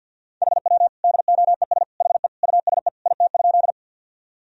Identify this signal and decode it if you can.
Morse code, heard as a keyed tone, about 50 wpm, 700 Hz. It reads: HW DOES HE USE IT?
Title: Sentences from Top 200 Words and Abbreviations 50wpm